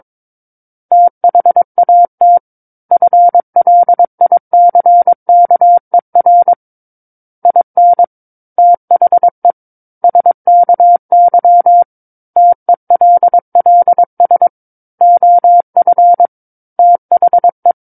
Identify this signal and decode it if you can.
Morse code, heard as a keyed tone, about 22 wpm, 700 Hz. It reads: THAT FLICKER IN THE SKY TELLS OF THE
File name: Book One / 13-Chapter 13 - 22 WPM